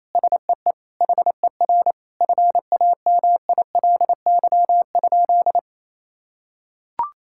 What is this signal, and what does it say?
Morse code, heard as a keyed tone, about 28 wpm, 700 Hz. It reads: SEE HER FAMILY?
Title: Sentences from Top 1000 Words - Review Part 6 - 28wpm